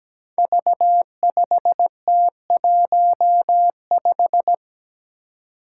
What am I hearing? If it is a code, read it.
Morse code, heard as a keyed tone, about 17 wpm, 700 Hz. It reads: V5T15
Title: Five Challenging Letter-Number Combinations 17wpm